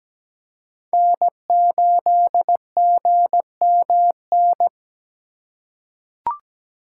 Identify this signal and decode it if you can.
Morse code, heard as a keyed tone, about 17 wpm, 700 Hz. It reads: N8GMN
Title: USA Call Signs - 5 Characters - 4x Character Spacing - 17wpm